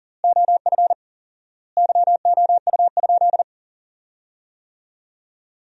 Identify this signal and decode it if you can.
Morse code, heard as a keyed tone, about 37 wpm, 700 Hz. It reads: OF YOU?
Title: 4x Word-Spacing - Sentences from Top 300 Words - 40wpm